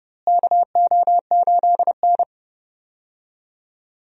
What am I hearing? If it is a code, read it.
Morse code, heard as a keyed tone, about 31 wpm, 700 Hz. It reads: KO8N
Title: USA Call Signs - 4 Characters - Repeated 3x - 30wpm